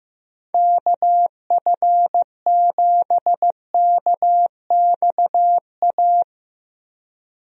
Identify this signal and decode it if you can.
Morse code, heard as a keyed tone, about 16 wpm, 700 Hz. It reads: KF7KXA
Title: USA Call Signs - 6 Characters - Repeated 3x - 15wpm